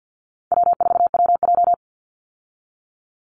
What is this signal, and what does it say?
Morse code, heard as a keyed tone, about 50 wpm, 700 Hz. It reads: W4WJ
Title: Easy Call Signs Repeated 3x - 50wpm